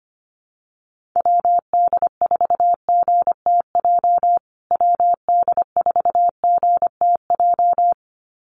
Morse code, heard as a keyed tone, about 25 wpm, 700 Hz. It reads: WD4GTJ WD4GTJ
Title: USA Call Signs Repeated 2x 25wpm